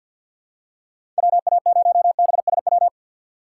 Morse code, heard as a keyed tone, about 53 wpm, 700 Hz. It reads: WA0BSW
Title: USA Call Signs - 6 Characters - 6x Character Spacing - 50wpm